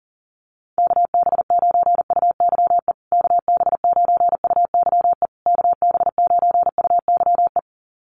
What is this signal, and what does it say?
Morse code, heard as a keyed tone, about 40 wpm, 700 Hz. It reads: KB9UYE KB9UYE KB9UYE
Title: USA Call Signs Repeated 3x 40wpm